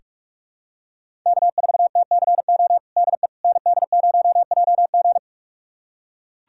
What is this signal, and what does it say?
Morse code, heard as a keyed tone, about 42 wpm, 700 Hz. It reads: K4TCO DE ND0JG
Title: USA Callsign DE Callsign - 5 Characters - 45wpm